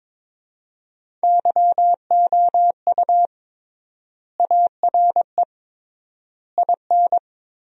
Morse code, heard as a keyed tone, about 22 wpm, 700 Hz. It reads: YOU ARE IN
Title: 3x Word-Spacing - Sentences from Top 500 Words - 22wpm